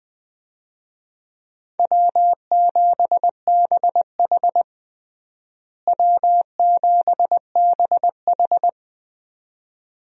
Morse code, heard as a keyed tone, about 20 wpm, 700 Hz. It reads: W7BH W7BH
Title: USA Call Signs - 4 Characters - Repeated 2x - 20wpm